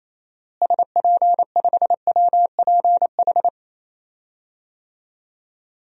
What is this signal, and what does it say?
Morse code, heard as a keyed tone, about 28 wpm, 700 Hz. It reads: SP5WPH
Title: DXCC-100 Call Signs - 6 Characters - Repeated 3x - 28wpm